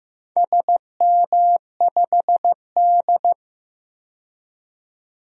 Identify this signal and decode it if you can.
Morse code, heard as a keyed tone, about 15 wpm, 700 Hz. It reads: SM5D
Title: DXCC-100 Call Signs - 4 Characters - 15wpm